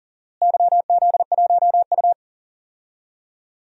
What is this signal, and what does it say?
Morse code, heard as a keyed tone, about 40 wpm, 700 Hz. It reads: YZ1U
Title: DXCC-100 Call Signs - 4 Characters - Repeated 2x - 40wpm